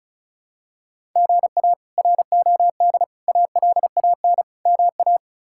Morse code, heard as a keyed tone, about 36 wpm, 700 Hz. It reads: GA ROD ALAN MA
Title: K1USN SST QSO Practice - 35wpm